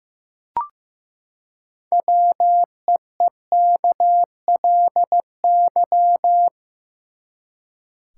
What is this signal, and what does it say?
Morse code, heard as a keyed tone, about 15 wpm, 700 Hz. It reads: WEEKLY
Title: Six Letter Words - 15wpm